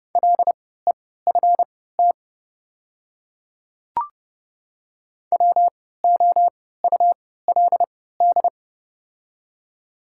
Morse code, heard as a keyed tone, about 30 wpm, spaced as for 17 wpm, 700 Hz. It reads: LEFT WOULD
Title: 3x Character Spacing - Top 300 Words - 30wpm